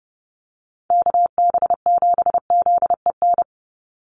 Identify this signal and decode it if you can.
Morse code, heard as a keyed tone, about 30 wpm, 700 Hz. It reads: KB7ZEN
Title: USA Call Signs 30wpm